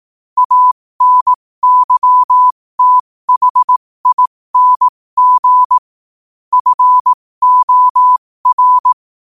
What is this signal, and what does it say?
Morse code, heard as a keyed tone, about 18 wpm, spaced as for 15 wpm, 1000 Hz. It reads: ANYTHING FOR
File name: 15wpm